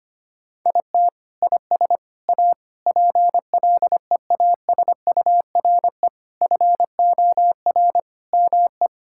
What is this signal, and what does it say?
Morse code, heard as a keyed tone, about 25 wpm, 700 Hz. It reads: IT IS A PLEASURE FOR ME
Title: Sentences from Top 1600 Words 25wpm